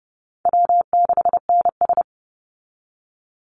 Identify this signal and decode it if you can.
Morse code, heard as a keyed tone, about 30 wpm, 700 Hz. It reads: W6NS
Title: Easy Call Signs Repeated 2x - 30wpm